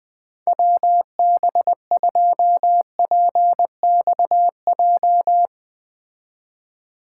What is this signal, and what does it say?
Morse code, heard as a keyed tone, about 20 wpm, 700 Hz. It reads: WB2PXJ
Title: Speed Racing - USA 2x3 Call Signs 17wpm